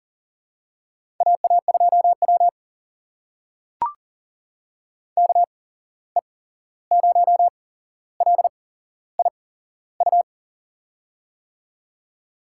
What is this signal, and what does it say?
Morse code, heard as a keyed tone, about 38 wpm, spaced as for 15 wpm, 700 Hz. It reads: AA2W KE0LIU
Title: USA Call Signs - 8x Character Spacing - 40wpm